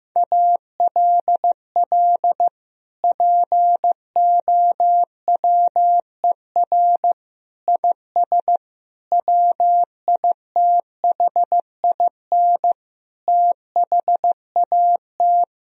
Morse code, heard as a keyed tone, about 15 wpm, 700 Hz. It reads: ALL POWER IS WITHIN THAT
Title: Sentences from Top 1000 Words - Review Part 4 - 15wpm